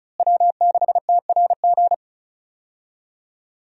Morse code, heard as a keyed tone, about 36 wpm, 700 Hz. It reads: W6TRG
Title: USA Call Signs - 5 Characters - Repeated 2x - 35wpm